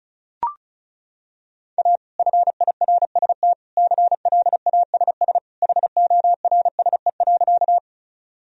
Morse code, heard as a keyed tone, about 36 wpm, 700 Hz. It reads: A FIRST CLASS HORSE.
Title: Sentences from Top 1000 Words - Review Part 2 - 35wpm